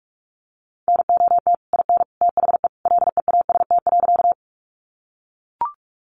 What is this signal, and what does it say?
Morse code, heard as a keyed tone, about 44 wpm, 700 Hz. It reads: NOT IN THE LEAST.
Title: Sentences from Top 700 Words 45wpm